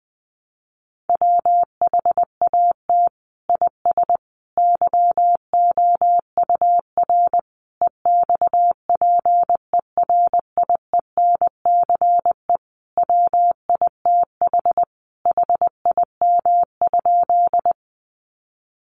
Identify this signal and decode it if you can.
Morse code, heard as a keyed tone, about 19 wpm, 700 Hz. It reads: WHAT IS YOUR EXPERIENCE WITH HIM?
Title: Sentences from Top 600 Words 20wpm